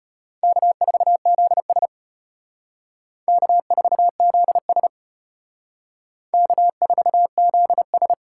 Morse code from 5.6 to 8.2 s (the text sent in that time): K4ZS